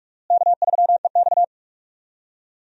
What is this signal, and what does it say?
Morse code, heard as a keyed tone, about 46 wpm, 700 Hz. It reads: K3EX
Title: USA Call Signs - 4 Characters - Repeated 3x - 45wpm